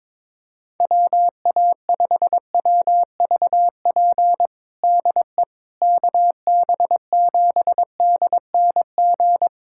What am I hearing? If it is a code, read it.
Morse code, heard as a keyed tone, about 22 wpm, 700 Hz. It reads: WA5WVP DE KB7DNG
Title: USA Callsign DE Callsign - 6 Characters - 22wpm